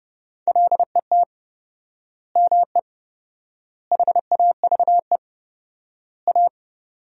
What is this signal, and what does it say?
Morse code, heard as a keyed tone, about 32 wpm, 700 Hz. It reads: LET ME HAVE A
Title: 4x Word-Spacing - Sentences from Top 300 Words - 30wpm